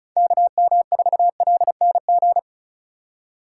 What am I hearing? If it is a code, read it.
Morse code, heard as a keyed tone, about 36 wpm, 700 Hz. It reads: KM4LNG
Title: USA Call Signs - 6x Character Spacing - 35wpm